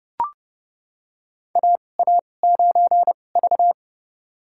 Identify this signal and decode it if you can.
Morse code, heard as a keyed tone, about 30 wpm, spaced as for 22 wpm, 690 Hz. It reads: AA9V
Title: USA Call Signs - 4 Characters - 2x Character Spacing - 30wpm